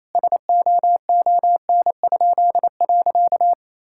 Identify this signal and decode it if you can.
Morse code, heard as a keyed tone, about 28 wpm, 700 Hz. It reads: SOON?.
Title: Pangram Challenge v2 - 28wpm